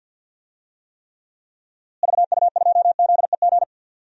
Morse code, heard as a keyed tone, about 53 wpm, 700 Hz. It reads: VU2ZEG
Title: DXCC-100 Call Signs - 6 Characters - Repeated 3x - 50wpm